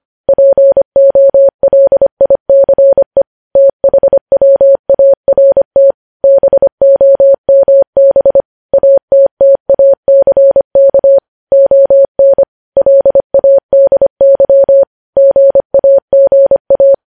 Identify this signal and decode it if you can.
Morse code, heard as a keyed tone, about 25 wpm, 550 Hz. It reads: POLICE THWART BOMB ATTACK ON LADY GAGA
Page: News Headlines in Morse Code at 25 WPM